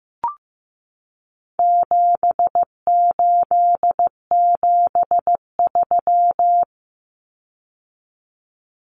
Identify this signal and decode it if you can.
Morse code, heard as a keyed tone, about 15 wpm, 700 Hz. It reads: 7873
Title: Common Cut Numbers - 4-Digit 15wpm